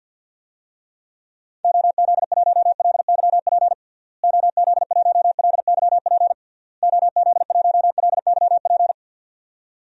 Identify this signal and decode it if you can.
Morse code, heard as a keyed tone, about 52 wpm, 700 Hz. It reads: OZ1LYP OZ1LYP OZ1LYP
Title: DXCC-100 Call Signs - 6 Characters - Repeated 3x - 50wpm